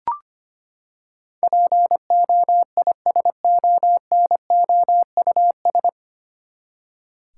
Morse code, heard as a keyed tone, about 25 wpm, 700 Hz. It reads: POISONOUS
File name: Nine Letter Words - 25wpm